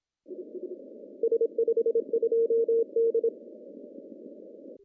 Morse code, heard as a keyed tone, about 26 wpm, 460 Hz. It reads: S52D
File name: CW / cqww-cw-k5tr